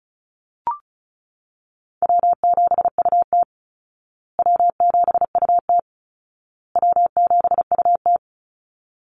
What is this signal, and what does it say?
Morse code, heard as a keyed tone, about 34 wpm, 700 Hz. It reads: W7UT W7UT W7UT
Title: Call Signs Repeated 2x - 35wpm